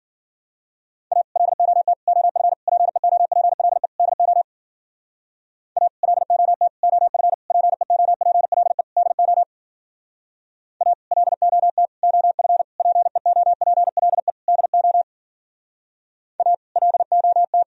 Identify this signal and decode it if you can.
Morse code, heard as a keyed tone, about 52 wpm, 700 Hz. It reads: A LOT OF PEOPLE DO A LOT OF PEOPLE DO A LOT OF PEOPLE DO A LOT